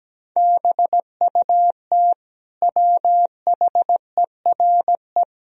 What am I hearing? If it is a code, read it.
Morse code, heard as a keyed tone, about 17 wpm, 700 Hz. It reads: BUT WHERE
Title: Sentences from Top 1000 Words - Review Part 2 - 17wpm